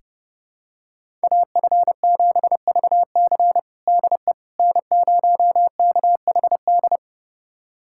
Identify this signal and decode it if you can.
Morse code, heard as a keyed tone, about 32 wpm, 700 Hz. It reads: AF7VC DE N0KHD
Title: USA Callsign DE Callsign - 5 Characters - 30wpm